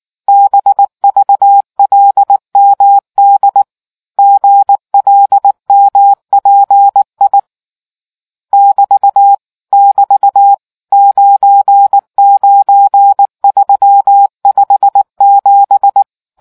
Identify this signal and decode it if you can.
Morse code, heard as a keyed tone, about 18 wpm, 800 Hz. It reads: BVLMD GLMPI <BT> <BT> 99357